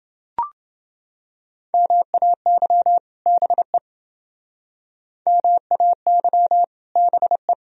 Morse code, heard as a keyed tone, about 28 wpm, 700 Hz. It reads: MAY BE MAY BE